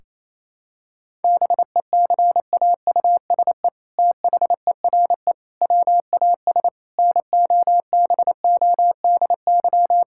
Morse code, heard as a keyed tone, about 28 wpm, 700 Hz. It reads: BECAUSE THERE WAS NOBODY
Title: Sentences from Top 1600 Words 28wpm